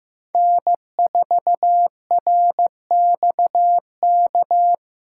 Morse code, heard as a keyed tone, about 15 wpm, 700 Hz. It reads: N4RXK
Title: USA Callsign DE Callsign - 5 Characters - 15wpm